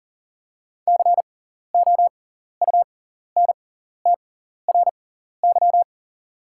Morse code, heard as a keyed tone, about 37 wpm, spaced as for 14 wpm, 690 Hz. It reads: COUNTRY
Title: 6x Character Spacing - Top 500 Words - 40wpm